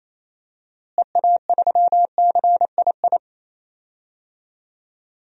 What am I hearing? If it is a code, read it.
Morse code, heard as a keyed tone, about 28 wpm, 700 Hz. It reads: EA3CII
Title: DXCC-100 Call Signs - 6 Characters - Repeated 2x - 28wpm